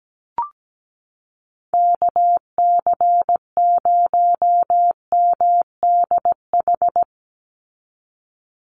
Morse code, heard as a keyed tone, about 17 wpm, 700 Hz. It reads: KC0MDH
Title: USA Call Signs 17wpm